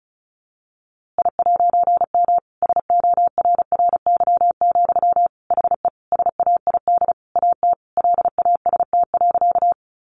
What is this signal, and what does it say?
Morse code, heard as a keyed tone, about 34 wpm, 700 Hz. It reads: I'M SORRY, HE SAID AT LAST.
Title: A Sound Of Thunder 35wpm